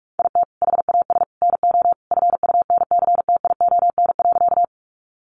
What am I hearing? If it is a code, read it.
Morse code, heard as a keyed tone, about 44 wpm, 700 Hz. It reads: IT HAS NO FUNCTION.
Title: Sentences from Top 1100 Words 45wpm